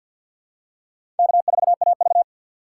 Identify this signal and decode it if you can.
Morse code, heard as a keyed tone, about 46 wpm, 690 Hz. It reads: K4AV
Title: USA Call Signs - 6x Character Spacing - 50wpm